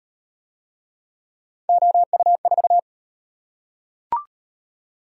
Morse code, heard as a keyed tone, about 35 wpm, 700 Hz. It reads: OU4